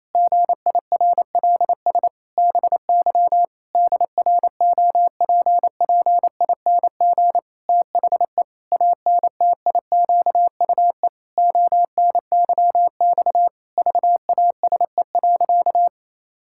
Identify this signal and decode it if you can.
Morse code, heard as a keyed tone, about 28 wpm, 700 Hz. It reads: GIRLS BY DROPPING THE ANTIQUE ONYX VASE.
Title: Pangram Challenge v2 - 28wpm